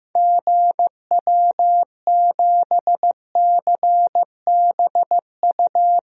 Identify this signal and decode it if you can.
Morse code, heard as a keyed tone, about 16 wpm, 700 Hz. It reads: GW7CBU
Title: DXCC-100 Call Signs - 6 Characters - Repeated 2x - 15wpm